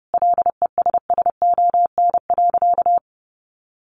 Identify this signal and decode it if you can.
Morse code, heard as a keyed tone, about 30 wpm, 700 Hz. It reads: LESSON.
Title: Sentences from Top 1200 Words 30wpm